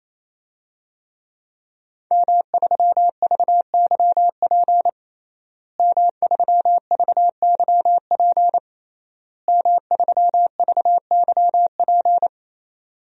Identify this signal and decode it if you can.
Morse code, heard as a keyed tone, about 27 wpm, 700 Hz. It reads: M3VYP M3VYP M3VYP
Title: DXCC-100 Call Signs - 5 Characters - Repeated 3x - 28wpm